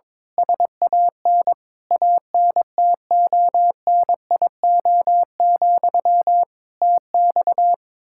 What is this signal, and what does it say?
Morse code, heard as a keyed tone, about 22 wpm, 700 Hz. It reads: SAN ANTONIO, TX
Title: Practice QSO - Exchange 1 - Beginning - 22wpm